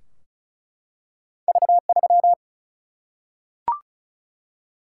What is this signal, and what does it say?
Morse code, heard as a keyed tone, about 36 wpm, 700 Hz. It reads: V3
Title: Single-Letter Single-Number 35wpm